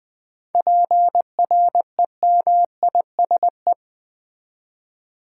Morse code, heard as a keyed tone, about 20 wpm, 700 Hz. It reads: PREMISE
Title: 7 Letter Words - First 3 Letters Given - 20wpm